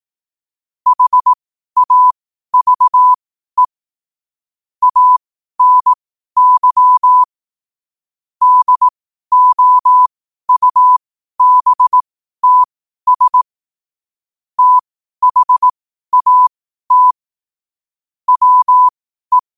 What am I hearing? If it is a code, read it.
Morse code, heard as a keyed tone, about 18 wpm, spaced as for 12 wpm, 1000 Hz. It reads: HAVE ANY DOUBTS THAT WE
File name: qotdfiles / 12wpm